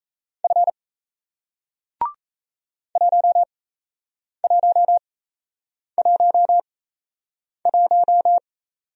Morse code, heard as a keyed tone, about 38 wpm, spaced as for 8 wpm, 700 Hz. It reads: F 1111